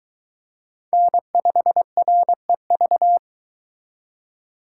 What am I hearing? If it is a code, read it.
Morse code, heard as a keyed tone, about 23 wpm, 700 Hz. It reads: N5REV